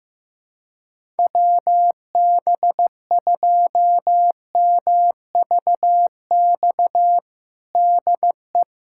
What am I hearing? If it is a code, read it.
Morse code, heard as a keyed tone, about 15 wpm, 700 Hz. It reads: WB2MVX DE
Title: USA Callsign DE Callsign - 6 Characters - 15wpm